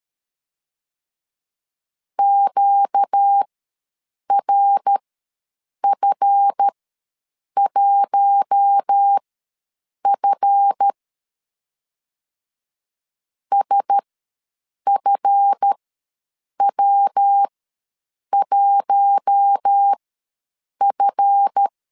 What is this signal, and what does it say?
Morse code, heard as a keyed tone, about 13 wpm, spaced as for 7 wpm, 800 Hz. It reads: QRF1F SFW1F
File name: LEZIONI / 7 GF9Z / 7 GF ws 12wpm 3140